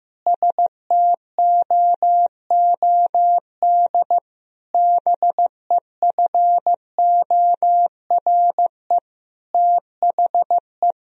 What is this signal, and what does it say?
Morse code, heard as a keyed tone, about 16 wpm, 700 Hz. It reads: STOOD BEFORE THE
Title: Sentences from Top 1000 Words - Review Part 2 - 15wpm